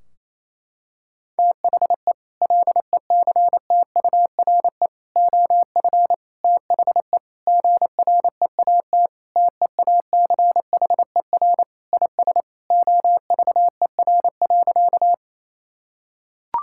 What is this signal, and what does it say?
Morse code, heard as a keyed tone, about 28 wpm, 700 Hz. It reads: THE LECTURE OF THE GREAT TEACHER IS OVER.
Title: Sentences from Top 1600 Words 28wpm